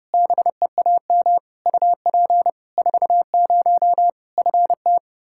Morse code, heard as a keyed tone, about 30 wpm, 700 Hz. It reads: BEAM UP 40 FT
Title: Practice QSO - Exchange 2 - Info - 30wpm